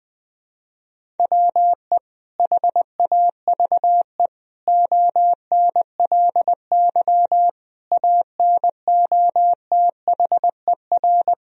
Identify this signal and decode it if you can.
Morse code, heard as a keyed tone, about 20 wpm, 700 Hz. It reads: WE HAVE ONLY ANOTHER
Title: Sentences from Top 1000 Words - Review Part 1 - 20wpm